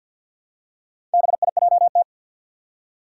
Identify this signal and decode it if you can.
Morse code, heard as a keyed tone, about 47 wpm, 700 Hz. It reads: BIJT
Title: Four Challenging Letter-Number Combinations 50wpm